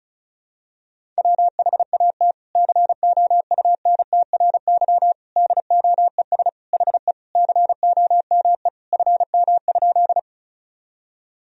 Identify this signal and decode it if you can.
Morse code, heard as a keyed tone, about 34 wpm, 700 Hz. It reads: WHAT COUNTRY DOES HE COME FM?
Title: Sentences from Top 300 Words and Abbreviations 35wpm